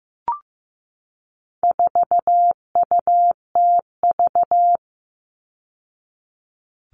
Morse code, heard as a keyed tone, about 15 wpm, 700 Hz. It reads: 4UTV
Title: All Cut Numbers - 4-Digit 15wpm